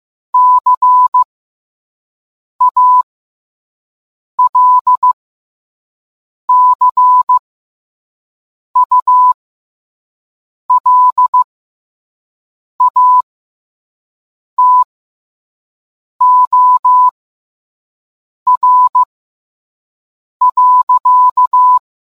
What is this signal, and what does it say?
Morse code, heard as a keyed tone, about 15 wpm, spaced as for 5 wpm, 1000 Hz. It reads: CALCULATOR.